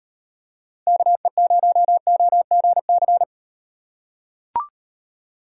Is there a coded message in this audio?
Yes: Morse code from 0.9 to 3.2 s, heard as a keyed tone, about 38 wpm, 700 Hz. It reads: KE0OGC